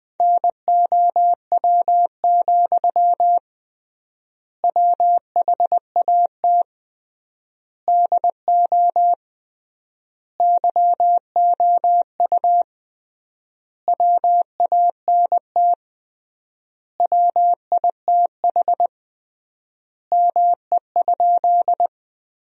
Morse code, heard as a keyed tone, about 21 wpm, 700 Hz. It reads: NOW, WHAT DO YOU WANT WITH ME?
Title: 3x Word-Spacing - Sentences from Top 200 Words - 20wpm